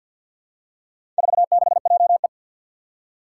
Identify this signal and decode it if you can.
Morse code, heard as a keyed tone, about 51 wpm, 700 Hz. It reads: 46JE
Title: Four Challenging Letter-Number Combinations 50wpm